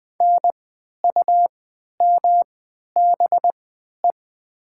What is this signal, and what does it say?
Morse code, heard as a keyed tone, about 20 wpm, spaced as for 11 wpm, 700 Hz. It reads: NUMBE
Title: 3x Character Spacing - Top 500 Words - 20wpm